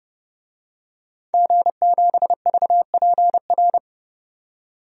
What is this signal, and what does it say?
Morse code, heard as a keyed tone, about 30 wpm, 700 Hz. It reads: G7VPR
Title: DXCC-100 Call Signs - 5 Characters - 30wpm